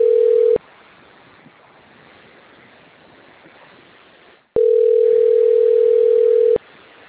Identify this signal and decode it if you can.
Ringback tone (North America), 440 + 480 Hz.
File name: out- - - - - .26572